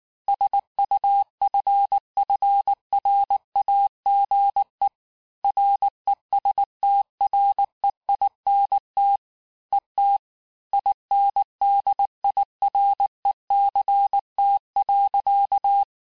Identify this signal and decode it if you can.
Morse code, heard as a keyed tone, about 19 wpm, 800 Hz. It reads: SUFFRAGE RESTREINT ET INDIRECT.
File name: cours-du-lundi files / cours-1595-teste-CLAIR-vitesse18-MPM-puis23MPM